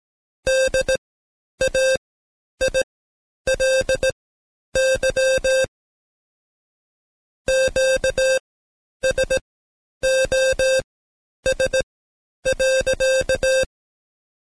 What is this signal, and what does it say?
Morse code, heard as a keyed tone, about 17 wpm, spaced as for 9 wpm, 530 Hz. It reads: DAILY QSOS.
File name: Code Quick / AudioDownld2 / Seg131-Stepper4 9WPM